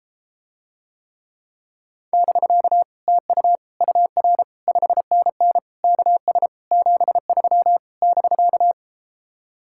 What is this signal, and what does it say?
Morse code, heard as a keyed tone, about 33 wpm, 700 Hz. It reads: <BK> TU UR 5NN KS 73 <BK>